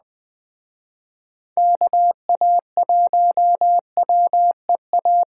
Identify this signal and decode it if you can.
Morse code, heard as a keyed tone, about 21 wpm, 700 Hz. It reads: KA1WEA